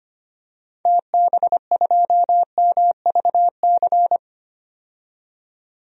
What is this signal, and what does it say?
Morse code, heard as a keyed tone, about 25 wpm, 690 Hz. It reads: TB2MVC
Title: DXCC-100 Call Signs - 6 Characters - 25wpm